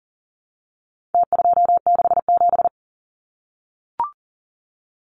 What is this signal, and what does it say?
Morse code, heard as a keyed tone, about 40 wpm, 700 Hz. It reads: T267
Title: Numbers - Contest Style - 4-Digit 40wpm